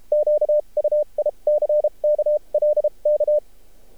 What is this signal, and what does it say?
Morse code, heard as a keyed tone, about 32 wpm, spaced as for 27 wpm, 600 Hz. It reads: QUICK LK
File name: archived / CRSnet